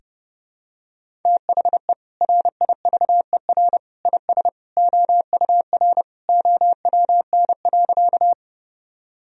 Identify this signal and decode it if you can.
Morse code, heard as a keyed tone, about 32 wpm, 700 Hz. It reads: THE RIVER IS OUR OWN.
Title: Sentences from Top 1000 Words - Review Part 1 - 30wpm